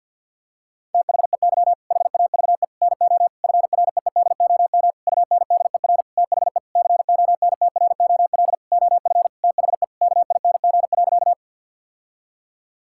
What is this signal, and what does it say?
Morse code, heard as a keyed tone, about 50 wpm, 700 Hz. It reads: THEY HAVE NO FREEDOM UNDER THE CONTROL OF THE KING.
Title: Sentences from Top 1000 Words - Review Part 5 - 50wpm